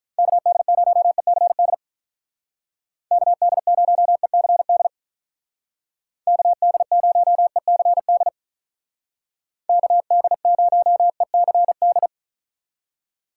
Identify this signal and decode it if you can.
Morse code, heard as a keyed tone, about 44 wpm, 700 Hz. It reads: KD0ECD KD0ECD KD0ECD KD0ECD